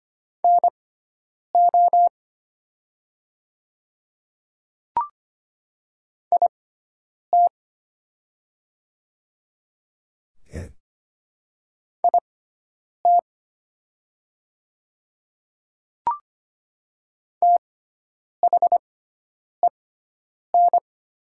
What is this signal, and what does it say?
Morse code, heard as a keyed tone, about 25 wpm, spaced as for 9 wpm, 700 Hz. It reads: NO IT IT THEN
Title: 6x Character Spacing - Top 100 Words - 25wpm